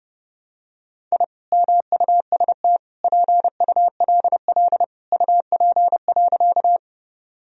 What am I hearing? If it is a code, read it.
Morse code, heard as a keyed tone, about 30 wpm, 700 Hz. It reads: I MUST PULL UP.
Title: Sentences from Top 1000 Words - Review Part 2 - 30wpm